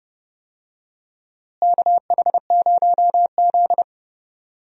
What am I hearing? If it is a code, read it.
Morse code, heard as a keyed tone, about 30 wpm, 700 Hz. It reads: KH0Z
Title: USA Call Signs - 4 Characters - Repeated 3x - 30wpm